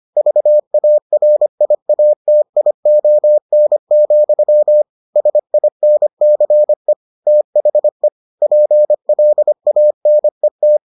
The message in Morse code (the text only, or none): VARIATION, SINCE THE PLANET